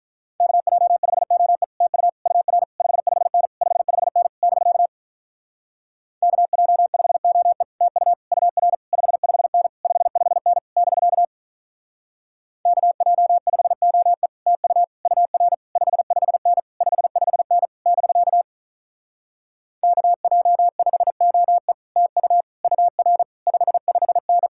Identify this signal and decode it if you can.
Morse code, heard as a keyed tone, about 44 wpm, 700 Hz. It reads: KJ5OE TU UR 55N 55N <BK> KJ5OE TU UR 55N 55N <BK> KJ5OE TU UR 55N 55N <BK> KJ5OE TU UR 55N